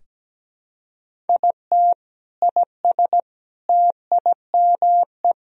Morse code, heard as a keyed tone, about 17 wpm, 700 Hz. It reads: IT IS TIME
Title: Sentences from Top 500 Words and Abbreviations 17wpm